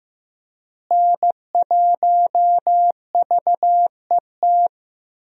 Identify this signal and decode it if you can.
Morse code, heard as a keyed tone, about 15 wpm, 700 Hz. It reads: N1VET
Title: USA Call Signs - 5 Characters - Repeated 3x - 15wpm